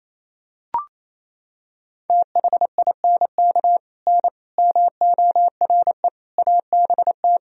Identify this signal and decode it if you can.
Morse code, heard as a keyed tone, about 28 wpm, 700 Hz. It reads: THINK N MORE ABT
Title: Sentences from Top 200 Words and Abbreviations 28wpm